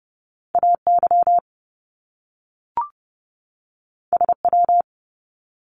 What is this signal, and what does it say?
Morse code, heard as a keyed tone, about 30 wpm, 700 Hz. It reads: AY SW
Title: Top Two Letter Combinations in Words 30wpm